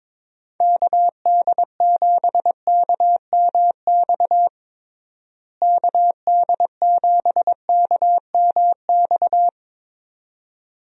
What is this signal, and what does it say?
Morse code, heard as a keyed tone, about 22 wpm, 700 Hz. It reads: KD7KMX KD7KMX
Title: USA Call Signs - 6 Characters - Repeated 2x - 22wpm